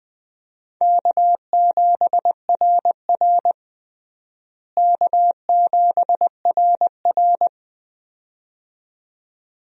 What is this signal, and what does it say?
Morse code, heard as a keyed tone, about 20 wpm, 700 Hz. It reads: K7RR K7RR
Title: USA Call Signs - 4 Characters - Repeated 2x - 20wpm